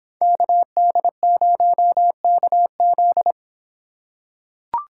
Morse code, heard as a keyed tone, about 26 wpm, 700 Hz. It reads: KD0KZ